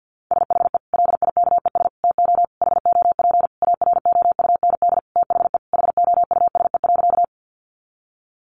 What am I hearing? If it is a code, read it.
Morse code, heard as a keyed tone, about 50 wpm, 700 Hz. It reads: SHE LIKES TO HOP AROUND THE HOUSE.
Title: Easy Phrases 50wpm